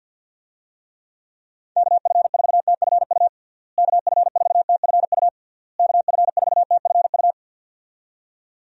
Morse code, heard as a keyed tone, about 46 wpm, 700 Hz. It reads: KF4TFU KF4TFU KF4TFU
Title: USA Call Signs - 6 Characters - Repeated 3x - 50wpm